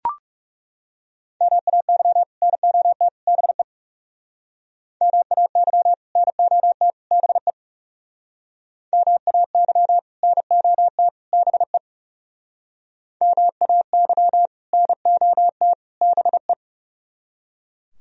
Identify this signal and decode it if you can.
Morse code, heard as a keyed tone, about 36 wpm, 700 Hz. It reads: MAY NOT BE MAY NOT BE MAY NOT BE MAY NOT BE
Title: Speed-Racing - Sets of 3 Words 30wpm